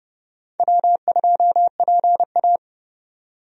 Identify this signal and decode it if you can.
Morse code, heard as a keyed tone, about 29 wpm, 700 Hz. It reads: W2PA
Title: USA Call Signs - 4 Characters - 30wpm